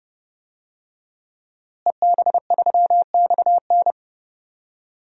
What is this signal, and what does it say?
Morse code, heard as a keyed tone, about 30 wpm, 700 Hz. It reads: EB3XN
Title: DXCC-100 Call Signs - 5 Characters - Repeated 3x - 30wpm